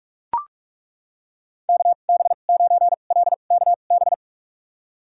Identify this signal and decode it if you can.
Morse code, heard as a keyed tone, about 41 wpm, spaced as for 31 wpm, 700 Hz. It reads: KB9LKB
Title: USA Call Signs - 6 Characters - 2x Character Spacing - 45wpm